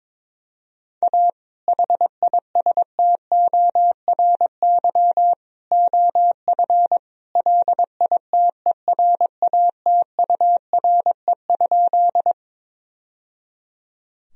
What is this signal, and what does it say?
Morse code, heard as a keyed tone, about 22 wpm, 700 Hz. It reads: A HISTORY OF LITERATURE?
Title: Sentences from Top 1000 Words - Review Part 3 - 22wpm